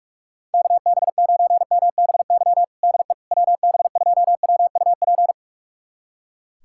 Morse code, heard as a keyed tone, about 45 wpm, 690 Hz. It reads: KB9MBY DE WB2WUP
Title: USA Callsign DE Callsign - 6 Characters - 45wpm